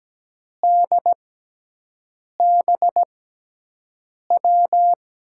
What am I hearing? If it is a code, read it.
Morse code, heard as a keyed tone, about 17 wpm, spaced as for 6 wpm, 700 Hz. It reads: DBW
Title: USA Call Signs - 6 Characters - 6x Character Spacing - 17wpm